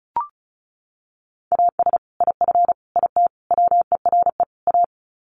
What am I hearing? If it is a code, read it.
Morse code, heard as a keyed tone, about 34 wpm, 700 Hz. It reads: AS IF IT WERE A
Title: Sets of 5 Words 35wpm